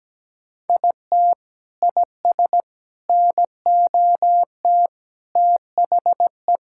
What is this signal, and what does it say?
Morse code, heard as a keyed tone, about 17 wpm, 700 Hz. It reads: IT IS NOT THE
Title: Sentences from Top 1000 Words - Review Part 6 - 17wpm